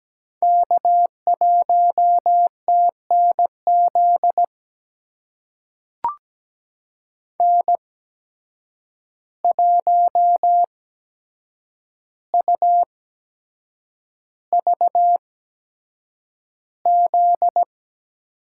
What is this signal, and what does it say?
Morse code, heard as a keyed tone, about 17 wpm, spaced as for 7 wpm, 700 Hz. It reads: K1TNZ N1UVZ
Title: USA Call Signs - 5 Characters - 8x Character Spacing - 17wpm